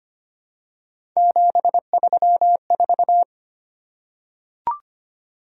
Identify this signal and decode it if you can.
Morse code, heard as a keyed tone, about 25 wpm, 700 Hz. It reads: 734
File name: Three Letter-Number Combinations - 25wpm